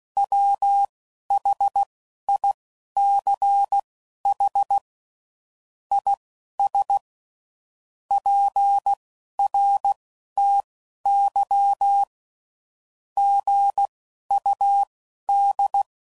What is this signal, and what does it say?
Morse code, heard as a keyed tone, about 16 wpm, spaced as for 11 wpm, 800 Hz. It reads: WHICH IS PRTY GUD
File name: CW Code Groups @10WPM / QSOs / QSO WA7QOX